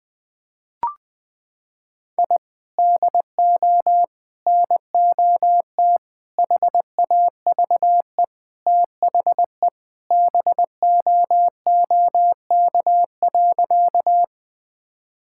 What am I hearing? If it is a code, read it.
Morse code, heard as a keyed tone, about 20 wpm, 700 Hz. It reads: I DO NOT HAVE THE BOOK.
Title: Sentences with 4-Word Phrases - 20wpm